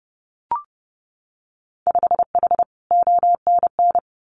Morse code, heard as a keyed tone, about 30 wpm, 700 Hz. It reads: 5H ONN
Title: Winter Field Day Exchanges Encore 30wpm